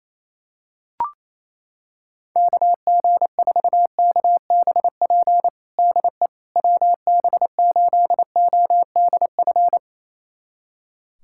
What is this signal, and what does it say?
Morse code, heard as a keyed tone, about 28 wpm, 700 Hz. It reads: KG4KBP DE WB8ODF
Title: USA Callsign DE Callsign 28wpm